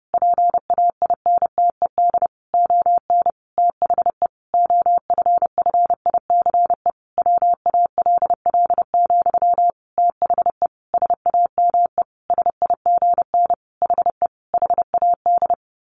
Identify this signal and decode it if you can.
Morse code, heard as a keyed tone, about 30 wpm, 700 Hz. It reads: PAINTED ON THE OFFICE WALL, THE SAME SIGN HE HAD
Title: A Sound Of Thunder 30wpm